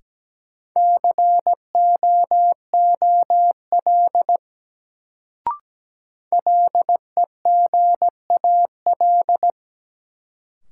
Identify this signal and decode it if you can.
Morse code, heard as a keyed tone, about 17 wpm, 700 Hz. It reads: COOL LEGAL
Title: Sentences from Top 1200 Words - New Word Drill Alt - 17wpm